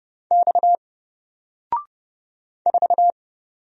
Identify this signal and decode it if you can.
Morse code, heard as a keyed tone, about 30 wpm, 700 Hz. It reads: X 4
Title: Single Letter-Number Alt 30wpm